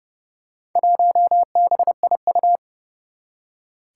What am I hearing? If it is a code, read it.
Morse code, heard as a keyed tone, about 30 wpm, 700 Hz. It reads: 1BIU
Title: Four Challenging Letter-Number Combinations 30wpm